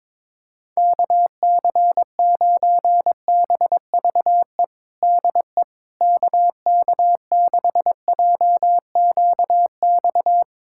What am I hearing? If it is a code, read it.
Morse code, heard as a keyed tone, about 22 wpm, 700 Hz. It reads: KC9BVE DE KK6JQX
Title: USA Callsign DE Callsign - 6 Characters - 22wpm